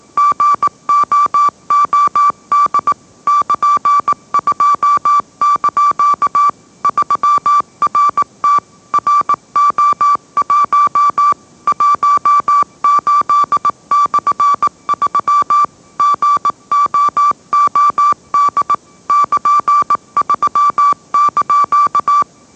Morse code, heard as a keyed tone, about 18 wpm, 1160 Hz. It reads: GOOD <KN>2) 3RT RO1 18/3 GOOD <KN>3)